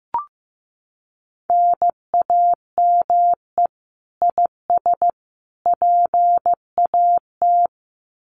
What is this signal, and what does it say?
Morse code, heard as a keyed tone, about 15 wpm, 700 Hz. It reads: NAME IS PAT
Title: Common QSO Phrases 15wpm